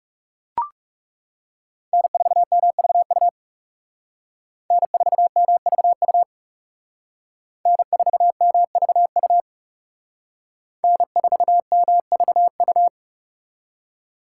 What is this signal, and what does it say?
Morse code, heard as a keyed tone, about 40 wpm, 700 Hz. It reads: N4MVU N4MVU N4MVU N4MVU